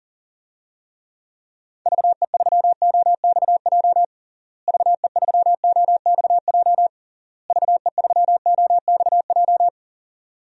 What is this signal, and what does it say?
Morse code, heard as a keyed tone, about 37 wpm, 700 Hz. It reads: VE3OXJ VE3OXJ VE3OXJ
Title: DXCC-100 Call Signs - 6 Characters - Repeated 3x - 40wpm